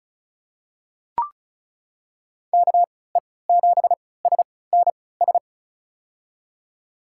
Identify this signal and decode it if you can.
Morse code, heard as a keyed tone, about 33 wpm, spaced as for 20 wpm, 700 Hz. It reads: KE7SNS
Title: USA Call Signs - 3x Character Spacing - 35wpm